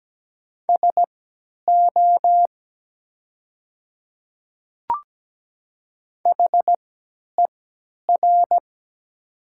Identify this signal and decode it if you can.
Morse code, heard as a keyed tone, about 17 wpm, spaced as for 10 wpm, 700 Hz. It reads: SO HER
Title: 3x Character Spacing - Top 200 Words - 17wpm